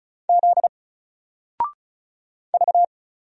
Morse code, heard as a keyed tone, about 36 wpm, 700 Hz. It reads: Z V
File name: Single-Letter-Number-35wpm